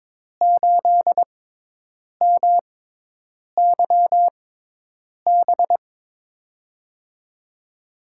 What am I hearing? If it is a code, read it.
Morse code, heard as a keyed tone, about 22 wpm, 700 Hz. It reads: 8 M Y B
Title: USA Call Signs - 6x Character Spacing - 22wpm